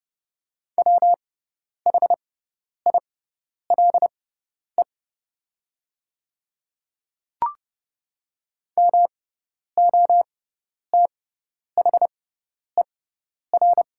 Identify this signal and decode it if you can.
Morse code, heard as a keyed tone, about 30 wpm, spaced as for 10 wpm, 700 Hz. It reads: WHILE MOTHER
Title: 6x Character Spacing - Top 300 Words - 30wpm